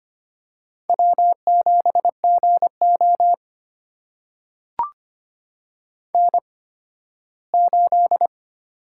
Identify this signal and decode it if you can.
Morse code, heard as a keyed tone, about 25 wpm, spaced as for 15 wpm, 700 Hz. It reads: W7GO N8
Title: USA Call Signs - 4 Characters - 8x Character Spacing - 25wpm